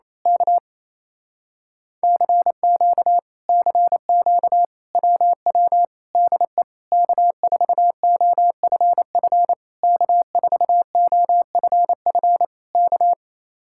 Morse code, heard as a keyed tone, about 26 wpm, 700 Hz. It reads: K CQ CQ WW DE K4OFF K4OFF K
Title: Speed-Racing Common QSO Phrases Encore - 28wpm